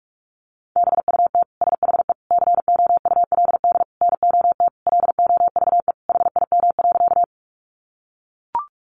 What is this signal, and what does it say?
Morse code, heard as a keyed tone, about 44 wpm, 700 Hz. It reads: BUT SHE COULD NOT LOVE HIM.
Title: Sentences from Top 600 Words 45wpm